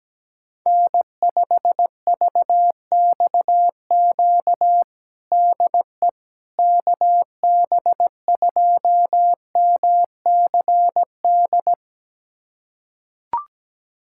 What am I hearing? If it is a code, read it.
Morse code, heard as a keyed tone, about 17 wpm, 700 Hz. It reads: N5VXQ DE KB2MCD
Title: USA Callsign DE Callsign 17wpm